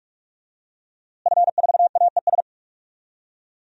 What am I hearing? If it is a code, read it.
Morse code, heard as a keyed tone, about 45 wpm, 700 Hz. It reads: F4AES
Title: DXCC-100 Call Signs - 5 Characters - Repeated 2x - 45wpm